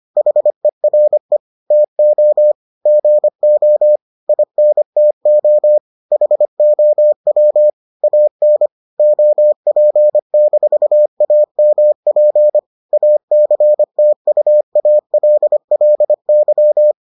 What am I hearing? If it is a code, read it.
Morse code, heard as a keyed tone, about 25 wpm, 600 Hz. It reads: HERE TO GO INTO HOW AN OP-AMP ACTUALLY